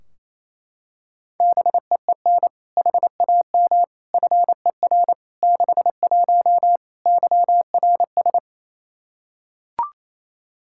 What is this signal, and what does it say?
Morse code, heard as a keyed tone, about 28 wpm, 700 Hz. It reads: BEEN HAM FER 61 YRS
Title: Practice QSO - Exchange 3 - More Info - 28wpm